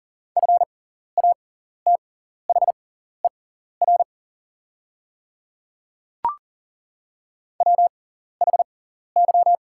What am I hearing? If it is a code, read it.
Morse code, heard as a keyed tone, about 37 wpm, spaced as for 14 wpm, 700 Hz. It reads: FATHER WHY
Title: 6x Character Spacing - Top 300 Words - 40wpm